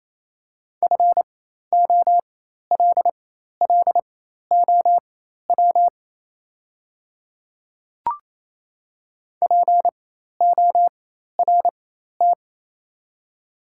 Morse code, heard as a keyed tone, about 28 wpm, spaced as for 13 wpm, 700 Hz. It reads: FOLLOW PORT
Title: 4x Character Spacing - Top 200 Words - 28wpm